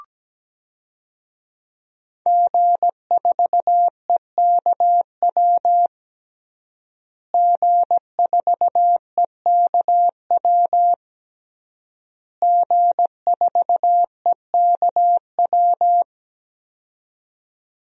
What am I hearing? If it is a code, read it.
Morse code, heard as a keyed tone, about 17 wpm, 700 Hz. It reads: G4EKW G4EKW G4EKW
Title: DXCC-100 Call Signs - 5 Characters - Repeated 3x - 17wpm